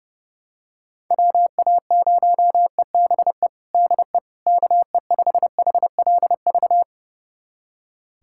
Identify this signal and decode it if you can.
Morse code, heard as a keyed tone, about 31 wpm, 700 Hz. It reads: WA0EBE DE KE5HLV
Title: USA Callsign DE Callsign - 6 Characters - 30wpm